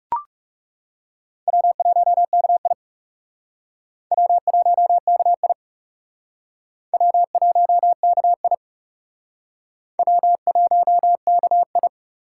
Morse code, heard as a keyed tone, about 38 wpm, 700 Hz. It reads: W1KI W1KI W1KI W1KI